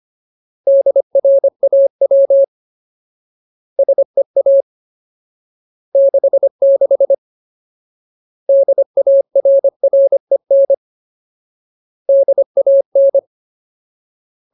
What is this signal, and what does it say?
Morse code, heard as a keyed tone, about 25 wpm, 550 Hz. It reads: DRAW SEA 66 DARREN DAN
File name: Ultimate Word List / 25WPM Ultimate